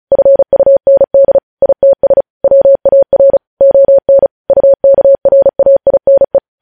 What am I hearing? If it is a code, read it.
Morse code, heard as a keyed tone, about 36 wpm, 550 Hz. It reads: FUND ITS WAR ON UKRAINE
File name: podcast / News headlines-1x-35x35